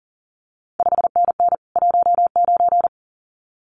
Morse code, heard as a keyed tone, about 38 wpm, 700 Hz. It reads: 5NN 19
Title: CQ WW DX Contest Exchange Encore - 40wpm